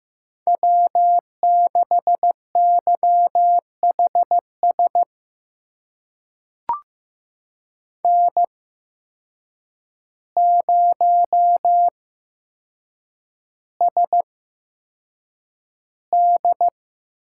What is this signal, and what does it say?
Morse code, heard as a keyed tone, about 15 wpm, spaced as for 7 wpm, 700 Hz. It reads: W6YHS N0SD
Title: USA Call Signs - 5 Characters - 8x Character Spacing - 15wpm